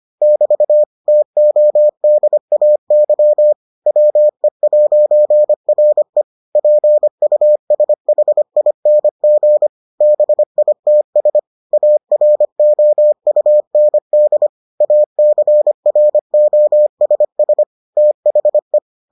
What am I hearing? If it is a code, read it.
Morse code, heard as a keyed tone, about 25 wpm, 600 Hz. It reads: <BT> TODAY WE'RE PUSHING BITS AROUND ACROSS THE